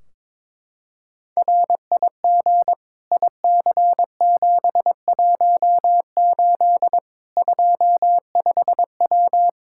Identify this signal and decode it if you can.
Morse code, heard as a keyed tone, about 22 wpm, 700 Hz. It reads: RIG IC718 25W
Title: K1USN SST QSO Practice 22wpm